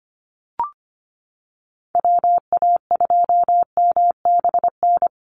Morse code, heard as a keyed tone, about 24 wpm, 700 Hz. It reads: WA2MBN
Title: USA Call Signs 25wpm